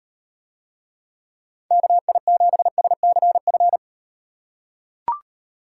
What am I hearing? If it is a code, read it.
Morse code, heard as a keyed tone, about 38 wpm, 700 Hz. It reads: KI7SCF